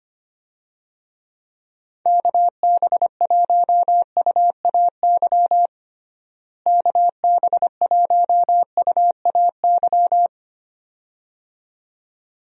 Morse code, heard as a keyed tone, about 25 wpm, 700 Hz. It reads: KB1UAY KB1UAY
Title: USA Call Signs - 6 Characters - Repeated 2x - 25wpm